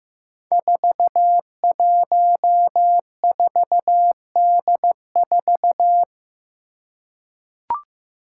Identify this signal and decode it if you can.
Morse code, heard as a keyed tone, about 15 wpm, 700 Hz. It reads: 414D4
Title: Five Challenging Letter-Number Combinations 15wpm